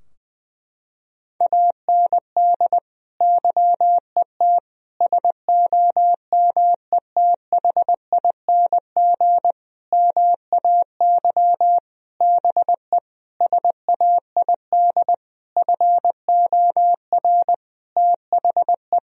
Morse code, heard as a keyed tone, about 21 wpm, 700 Hz. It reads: AND YET SOMETHING MAY BE SAID FOR THE
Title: Sentences from Top 1000 Words - Review Part 3 - 20wpm